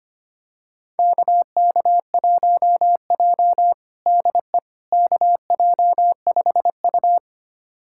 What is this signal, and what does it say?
Morse code, heard as a keyed tone, about 25 wpm, 700 Hz. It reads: KK1J DE KJ5U
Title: USA Callsign DE Callsign - 4 Characters - 25wpm